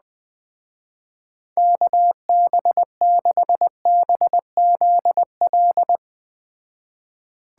Morse code, heard as a keyed tone, about 20 wpm, 700 Hz. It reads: KB6BZL
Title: Speed Racing - USA Call Signs 17wpm